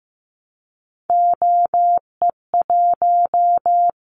Morse code, heard as a keyed tone, about 15 wpm, 700 Hz. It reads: OE1
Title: Callsign Prefixes for Top 100 DXCC - 15wpm